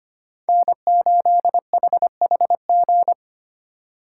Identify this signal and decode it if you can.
Morse code, heard as a keyed tone, about 25 wpm, 700 Hz. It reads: N8HHG
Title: K1USN SST QSO Practice - 25wpm